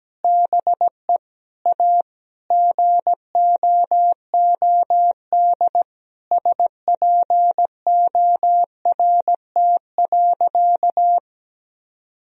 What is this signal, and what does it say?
Morse code, heard as a keyed tone, about 17 wpm, 700 Hz. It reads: BE A GOOD SPORT.
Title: Sentences from Top 1000 Words - Review Part 4 - 17wpm